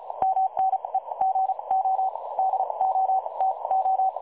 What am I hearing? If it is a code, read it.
Morse code, heard as a keyed tone, about 36 wpm, spaced as for 28 wpm, 760 Hz. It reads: MNE GO DOTO
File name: dlazaawansowanych / nagraniazpasma / websdr recording 3565.0kHz